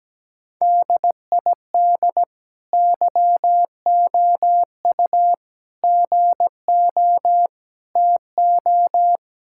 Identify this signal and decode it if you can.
Morse code, heard as a keyed tone, about 17 wpm, 700 Hz. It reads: DID YOU GO TO
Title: Sentences from Top 1000 Words - Review Part 5 - 17wpm